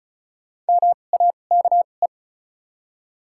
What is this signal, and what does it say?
Morse code, heard as a keyed tone, about 36 wpm, spaced as for 25 wpm, 700 Hz. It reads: MAKE
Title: 2x Character Spacing - Top 100 Words - 35wpm